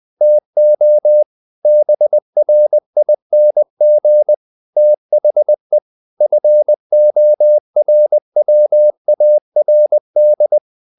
Morse code, heard as a keyed tone, about 20 wpm, 600 Hz. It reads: TO BRING THE FORWARD
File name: the old man and the sea part 2 20wpm